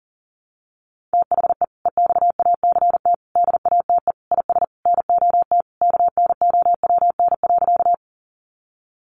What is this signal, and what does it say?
Morse code, heard as a keyed tone, about 38 wpm, 700 Hz. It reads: THE EXACT DATE IS NOT KNOWN.
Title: Sentences from Top 1300 Words 40wpm